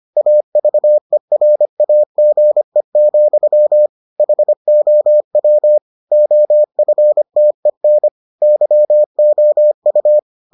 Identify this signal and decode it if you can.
Morse code, heard as a keyed tone, about 25 wpm, 600 Hz. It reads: AVERAGE, HOW OFTEN YOU